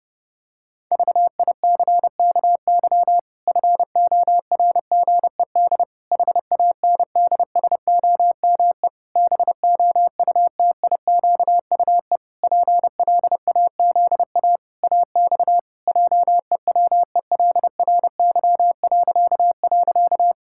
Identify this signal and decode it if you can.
Morse code, heard as a keyed tone, about 32 wpm, 700 Hz. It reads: VICKY FORGED HANDSOME BOUTIQUE PLAZA AX JEWELRY..
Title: Pangram Challenge v2 - 30wpm